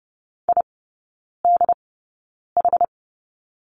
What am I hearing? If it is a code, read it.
Morse code, heard as a keyed tone, about 30 wpm, spaced as for 9 wpm, 700 Hz. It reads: IDH
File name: Lesson 17e - Letter u - Three Characters 30wpm